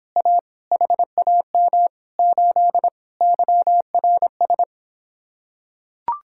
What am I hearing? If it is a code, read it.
Morse code, heard as a keyed tone, about 26 wpm, 700 Hz. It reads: A HAM 8 YRS